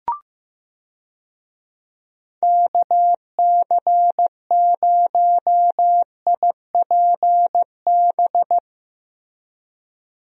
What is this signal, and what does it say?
Morse code, heard as a keyed tone, about 15 wpm, 700 Hz. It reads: KC0IPB
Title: USA Call Signs - 6 Characters - Repeated 2x - 15wpm